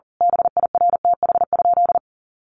Morse code, heard as a keyed tone, about 37 wpm, 700 Hz. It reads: BIRTH?
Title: Sentences from Top 800 Words 40wpm